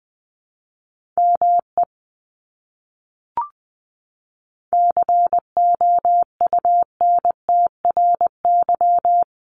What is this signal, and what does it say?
Morse code, heard as a keyed tone, about 20 wpm, 700 Hz. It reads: ME COUNTRY
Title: Top 300 Words Repeated 20wpm